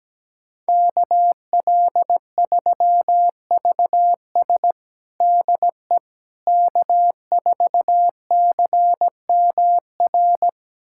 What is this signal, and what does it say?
Morse code, heard as a keyed tone, about 17 wpm, 700 Hz. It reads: KL3VS DE K4CMR
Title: USA Callsign DE Callsign - 5 Characters - 17wpm